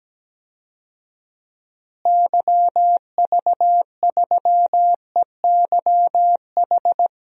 Morse code, heard as a keyed tone, about 17 wpm, 700 Hz. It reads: YV3EYH
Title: DXCC-100 Call Signs - 6 Characters - Repeated 2x - 17wpm